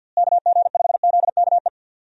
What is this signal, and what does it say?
Morse code, heard as a keyed tone, about 52 wpm, 700 Hz. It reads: KG5ZKE
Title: USA Call Signs - 6 Characters - Repeated 2x - 50wpm